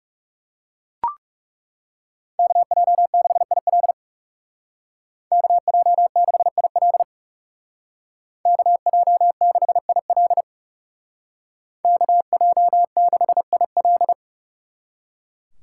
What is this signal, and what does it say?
Morse code, heard as a keyed tone, about 39 wpm, 700 Hz. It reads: KJ6IL KJ6IL KJ6IL KJ6IL